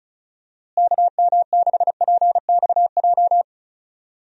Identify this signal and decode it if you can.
Morse code, heard as a keyed tone, about 34 wpm, 700 Hz. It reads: KM6PXJ
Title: USA Call Signs - 6 Characters - Repeated 3x - 35wpm